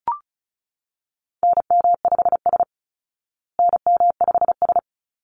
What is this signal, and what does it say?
Morse code, heard as a keyed tone, about 34 wpm, 700 Hz. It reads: NM5S NM5S
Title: Call Signs Repeated 3x - 35wpm